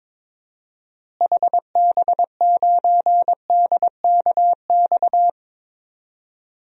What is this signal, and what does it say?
Morse code, heard as a keyed tone, about 22 wpm, 700 Hz. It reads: HB9DKX
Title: DXCC-100 Call Signs - 6 Characters - Repeated 3x - 22wpm